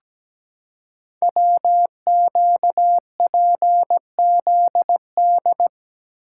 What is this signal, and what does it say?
Morse code, heard as a keyed tone, about 17 wpm, 700 Hz. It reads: WQPZD
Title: Code Groups - 5 Letter:Numbers - 17wpm